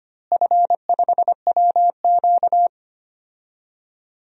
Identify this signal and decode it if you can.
Morse code, heard as a keyed tone, about 25 wpm, 700 Hz. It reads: F5WQ
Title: DXCC-100 Call Signs - 4 Characters - Repeated 2x - 25wpm